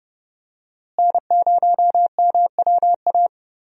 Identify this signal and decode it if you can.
Morse code, heard as a keyed tone, about 30 wpm, 700 Hz. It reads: N0MWA
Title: USA Call Signs - 5 Characters - Repeated 2x - 30wpm